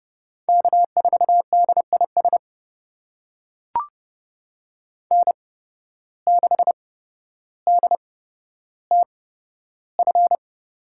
Morse code, heard as a keyed tone, about 30 wpm, spaced as for 13 wpm, 700 Hz. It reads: K4DIS N6DTF
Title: USA Call Signs - 5 Characters - 8x Character Spacing - 30wpm